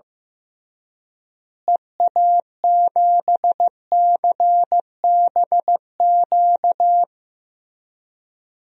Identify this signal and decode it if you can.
Morse code, heard as a keyed tone, about 16 wpm, 700 Hz. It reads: EA7CBQ
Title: DXCC-100 Call Signs - 6 Characters - Repeated 3x - 15wpm